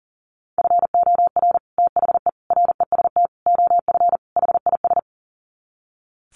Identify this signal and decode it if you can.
Morse code, heard as a keyed tone, about 40 wpm, 700 Hz. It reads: FOR THE REST OF HIS
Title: Sets of 5 Words 40wpm